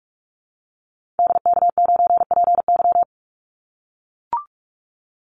Morse code, heard as a keyed tone, about 44 wpm, 700 Hz. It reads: DK9PY
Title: Call Signs - 45wpm